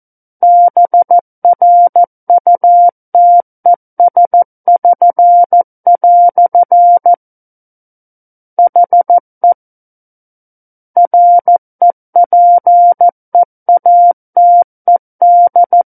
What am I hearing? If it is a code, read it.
Morse code, heard as a keyed tone, about 14 wpm, 700 Hz. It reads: BRUTES<SN>" HE REPEATED
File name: Book One / 05-Chapter 5 - 14 WPM